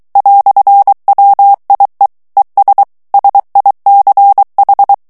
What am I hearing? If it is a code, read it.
Morse code, heard as a keyed tone, about 23 wpm, 800 Hz. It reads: "WIE ES SICH
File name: morse mp3 / ALLEMAND texte clair 1 vitesse 22 mots minute